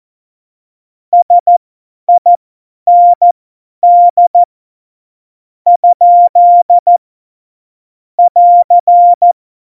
Morse code, heard as a keyed tone, about 14 wpm, spaced as for 10 wpm, 700 Hz. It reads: SIND ? <AR>
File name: Morsekurs / Witze / Witz 15. September 2020, 50 F70